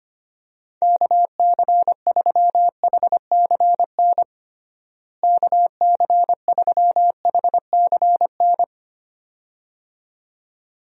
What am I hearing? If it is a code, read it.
Morse code, heard as a keyed tone, about 25 wpm, 700 Hz. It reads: KC3HCN KC3HCN
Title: USA Call Signs - 6 Characters - Repeated 2x - 25wpm